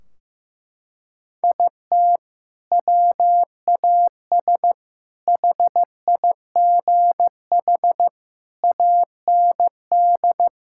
Morse code, heard as a keyed tone, about 15 wpm, 700 Hz. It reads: IT WAS HIGH AND
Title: Sentences from Top 1600 Words 15wpm